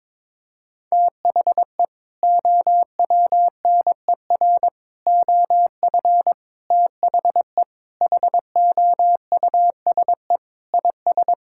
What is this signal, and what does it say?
Morse code, heard as a keyed tone, about 22 wpm, 700 Hz. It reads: THE OWNER OF THE HOUSE IS
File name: practice / 22wpm / Sentences with 3-Word Phrases - 22wpm